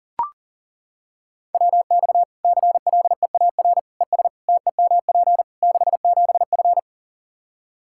Morse code, heard as a keyed tone, about 40 wpm, 700 Hz. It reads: WX CLEAR ES TEMP 67F
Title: K1USN SST QSO Practice 40wpm